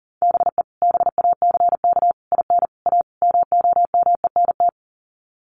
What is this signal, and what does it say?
Morse code, heard as a keyed tone, about 40 wpm, 700 Hz. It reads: BE BACK IN A MOMENT
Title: Sets of 5 Words 40wpm